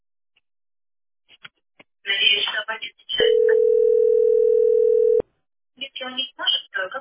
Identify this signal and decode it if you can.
Ringback tone (North America), 440 + 480 Hz, with speech.
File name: 06fcaa6f-0f91-4d3f-a866